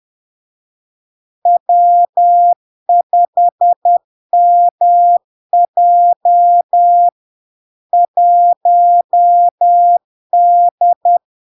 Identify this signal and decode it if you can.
Morse code, heard as a keyed tone, about 10 wpm, 700 Hz. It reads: W5MJ 1D
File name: LARK-FD-2022-Log-in-CW-10wpm-700-Hz